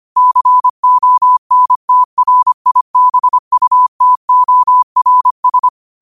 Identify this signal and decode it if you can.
Morse code, heard as a keyed tone, about 25 wpm, 1000 Hz. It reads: CONTRIBUTORS
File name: qotdfiles / 25wpm